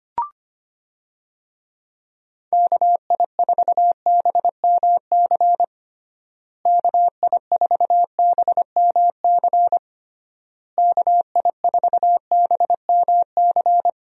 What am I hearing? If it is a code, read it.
Morse code, heard as a keyed tone, about 25 wpm, 700 Hz. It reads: KI4BMC KI4BMC KI4BMC
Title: USA Call Signs - 6 Characters - Repeated 3x - 25wpm